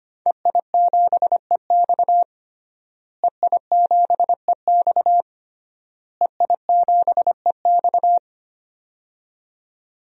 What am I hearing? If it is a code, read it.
Morse code, heard as a keyed tone, about 25 wpm, 700 Hz. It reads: EI7EX EI7EX EI7EX
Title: DXCC-100 Call Signs - 5 Characters - Repeated 3x - 25wpm